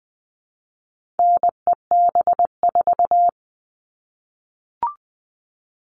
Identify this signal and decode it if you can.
Morse code, heard as a keyed tone, about 20 wpm, 700 Hz. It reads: NEB4
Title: All Cut Numbers - 4-Digit 20wpm